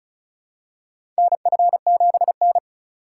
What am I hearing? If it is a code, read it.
Morse code, heard as a keyed tone, about 36 wpm, 700 Hz. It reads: NF7N
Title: USA Call Signs - 4 Characters - 3x Character Spacing - 35wpm